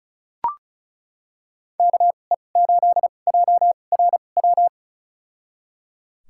Morse code, heard as a keyed tone, about 34 wpm, spaced as for 25 wpm, 700 Hz. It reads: KE8JRW
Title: USA Call Signs - 2x Character Spacing - 35wpm